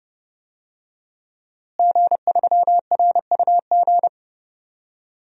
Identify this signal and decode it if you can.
Morse code, heard as a keyed tone, about 30 wpm, 700 Hz. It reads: G3RUG
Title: DXCC-100 Call Signs - 5 Characters - Repeated 2x - 30wpm